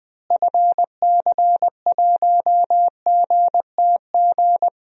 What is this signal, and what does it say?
Morse code, heard as a keyed tone, about 20 wpm, 700 Hz. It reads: FC1GTG
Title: DXCC-100 Call Signs - 6 Characters - Repeated 3x - 20wpm